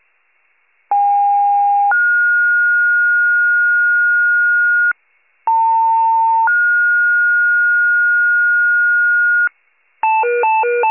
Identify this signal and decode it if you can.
Two-tone page, heard as a one-second tone, then a three-second tone, about 810 then 1430 Hz, about 900 then 1430 Hz. It is heard from 0.9 to 9.5 s.
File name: wcalert